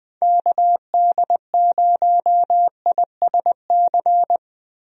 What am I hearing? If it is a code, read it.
Morse code, heard as a keyed tone, about 20 wpm, 700 Hz. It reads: KD0ISC
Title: USA Call Signs - 6 Characters - Repeated 3x - 20wpm